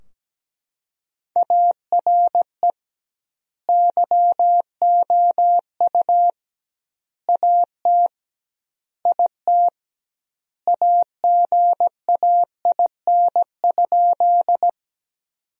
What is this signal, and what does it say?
Morse code, heard as a keyed tone, about 17 wpm, 700 Hz. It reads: ARE YOU AT IT AGAIN?
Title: 2x Word-Spacing - Sentences from Top 200 Words - 17wpm